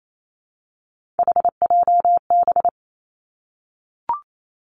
Morse code, heard as a keyed tone, about 28 wpm, 700 Hz. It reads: HJB
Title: Three Challenging Letter-Number Combinations 28wpm